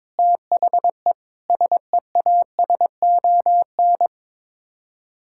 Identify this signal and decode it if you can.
Morse code, heard as a keyed tone, about 22 wpm, 700 Hz. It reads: THE SEASON
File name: practice / 22wpm / Sets of 6 Words - 22wpm